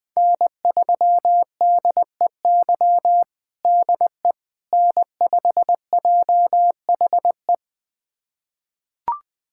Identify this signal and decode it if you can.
Morse code, heard as a keyed tone, about 20 wpm, 700 Hz. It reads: N3DEY DE N5JHE
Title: USA Callsign DE Callsign - 5 Characters - 20wpm